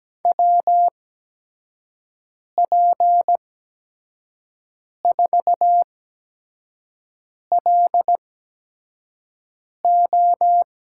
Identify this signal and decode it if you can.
Morse code, heard as a keyed tone, about 17 wpm, spaced as for 5 wpm, 700 Hz. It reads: WP4LO
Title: USA Call Signs - 6 Characters - 8x Character Spacing - 17wpm